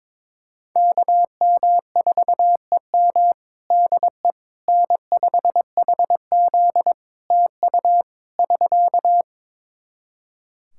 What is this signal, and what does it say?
Morse code, heard as a keyed tone, about 22 wpm, 700 Hz. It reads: KM4EM DE N5HZ TU <SK>
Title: Practice QSO - Exchange 4 - Ending - 22wpm